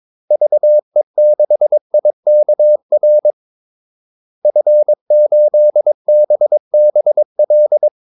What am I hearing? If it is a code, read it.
Morse code, heard as a keyed tone, about 22 wpm, 600 Hz. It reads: VE6IKR F8BBL
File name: Callsigns-22wpm